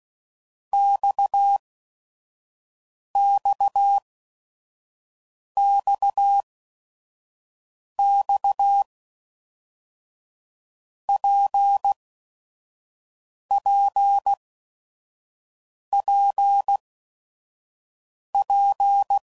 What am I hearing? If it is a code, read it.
Morse code, heard as a keyed tone, about 16 wpm, spaced as for 5 wpm, 800 Hz. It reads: XXXX PPPP
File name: Basic Letter Groups By7Groups / RKPX / RKPX@5 f15